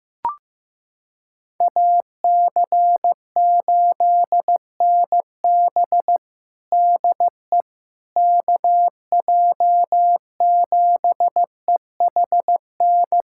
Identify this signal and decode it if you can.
Morse code, heard as a keyed tone, about 15 wpm, 700 Hz. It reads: AC8NB DE KJ7EHN
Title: USA Callsign DE Callsign 15wpm